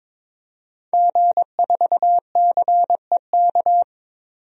Morse code, heard as a keyed tone, about 22 wpm, 700 Hz. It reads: G4CEK
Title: DXCC-100 Call Signs - 5 Characters - 22wpm